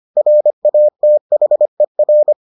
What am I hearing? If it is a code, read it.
Morse code, heard as a keyed tone, about 25 wpm, 600 Hz. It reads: RATHER